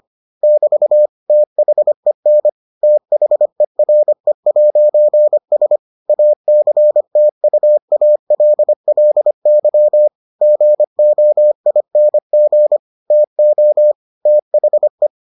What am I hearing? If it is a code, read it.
Morse code, heard as a keyed tone, about 25 wpm, 600 Hz. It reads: <BT> THEN THERE'S ACTUALLY GOING TO THE